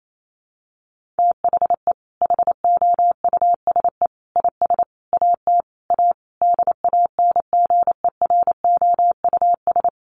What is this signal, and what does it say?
Morse code, heard as a keyed tone, about 28 wpm, 700 Hz. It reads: THE HOUSE IS AT A DANGEROUS
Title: Sentences from Top 1500 Words 28wpm